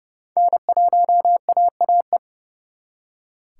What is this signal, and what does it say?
Morse code, heard as a keyed tone, about 36 wpm, 700 Hz. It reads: N1AAE
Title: Speed Racing - USA 1x3 Call Signs 30wpm